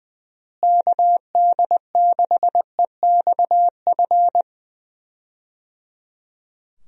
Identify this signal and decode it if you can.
Morse code, heard as a keyed tone, about 20 wpm, 700 Hz. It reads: KD6EXF
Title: USA Call Signs - 6 Characters - Repeated 3x - 20wpm